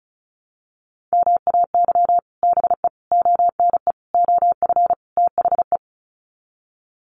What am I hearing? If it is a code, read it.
Morse code, heard as a keyed tone, about 34 wpm, 700 Hz. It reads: MAY BE ONE OF THE
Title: Sets of 5 Words Encore2 35wpm